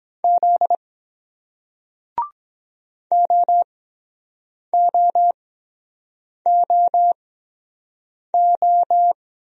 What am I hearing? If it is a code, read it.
Morse code, heard as a keyed tone, about 23 wpm, spaced as for 7 wpm, 700 Hz. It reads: Z OOOO